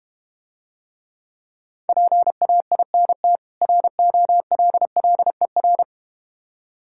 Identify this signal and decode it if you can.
Morse code, heard as a keyed tone, about 32 wpm, 700 Hz. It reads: PAINT ROLLER